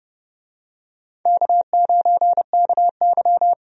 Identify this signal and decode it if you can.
Morse code, heard as a keyed tone, about 30 wpm, 700 Hz. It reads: K9KY
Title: USA Call Signs - 4 Characters - Repeated 2x - 30wpm